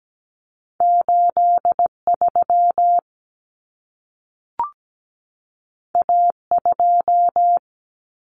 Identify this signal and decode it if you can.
Morse code, heard as a keyed tone, about 17 wpm, 700 Hz. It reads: 83 A2
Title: Common Cut Numbers - 2-Digit 17wpm